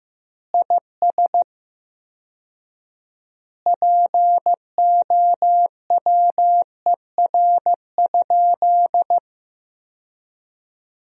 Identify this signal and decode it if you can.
Morse code, heard as a keyed tone, about 15 wpm, 700 Hz. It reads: IS POWER?
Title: 4x Word-Spacing - Sentences from Top 500 Words - 15wpm